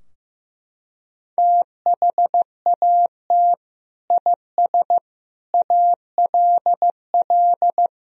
Morse code, heard as a keyed tone, about 15 wpm, 700 Hz. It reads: THAT IS ALL
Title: Sentences from Top 1000 Words - Review Part 2 - 15wpm